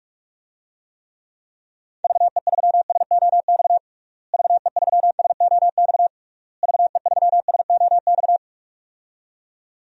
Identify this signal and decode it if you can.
Morse code, heard as a keyed tone, about 46 wpm, 700 Hz. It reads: VE3SOX VE3SOX VE3SOX
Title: DXCC-100 Call Signs - 6 Characters - Repeated 3x - 45wpm